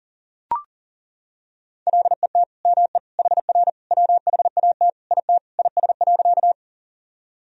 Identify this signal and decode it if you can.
Morse code, heard as a keyed tone, about 40 wpm, 700 Hz. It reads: LET ME HR WHAT IT IS.
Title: Sentences from Top 300 Words and Abbreviations 40wpm